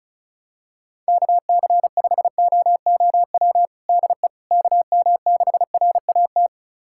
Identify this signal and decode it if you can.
Morse code, heard as a keyed tone, about 36 wpm, 700 Hz. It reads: KC5OOW DE KM6RAT
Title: USA Callsign DE Callsign - 6 Characters - 35wpm